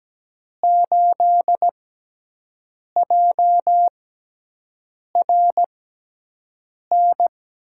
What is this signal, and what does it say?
Morse code, heard as a keyed tone, about 17 wpm, spaced as for 6 wpm, 700 Hz. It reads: 8JRN
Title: USA Call Signs - 5 Characters - 6x Character Spacing - 17wpm